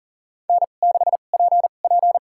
Morse code, heard as a keyed tone, about 40 wpm, spaced as for 29 wpm, 700 Hz. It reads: N6PP
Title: USA Call Signs - 4 Characters - 2x Character Spacing - 40wpm